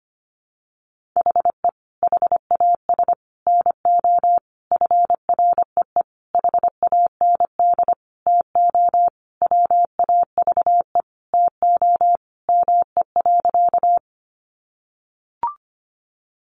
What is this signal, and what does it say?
Morse code, heard as a keyed tone, about 25 wpm, 700 Hz. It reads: HE HAS NO FREE HAND TO WAVE TO ME.
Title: Sentences from Top 1400 Words 25wpm